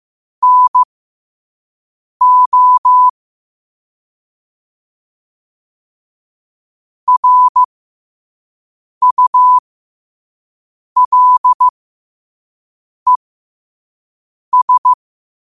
Morse code, heard as a keyed tone, about 15 wpm, spaced as for 5 wpm, 1000 Hz. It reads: NO RULES
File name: qotdfiles / 5wpm